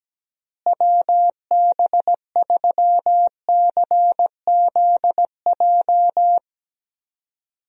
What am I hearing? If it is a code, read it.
Morse code, heard as a keyed tone, about 17 wpm, 700 Hz. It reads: WB3CZJ
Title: USA Call Signs - 6 Characters - Repeated 2x - 17wpm